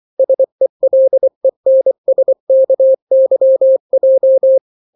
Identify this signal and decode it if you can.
Morse code, heard as a keyed tone, about 24 wpm, 530 Hz. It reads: SELENSKYJ
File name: daily / 23wpm